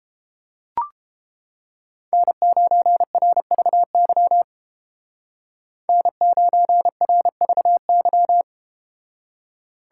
Morse code, heard as a keyed tone, about 31 wpm, 700 Hz. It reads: N9RVY N9RVY